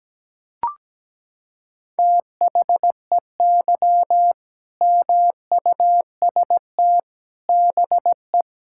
Morse code, heard as a keyed tone, about 17 wpm, 700 Hz. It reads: THEY MUST BE
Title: Sentences from Top 1000 Words - Review Part 5 - 17wpm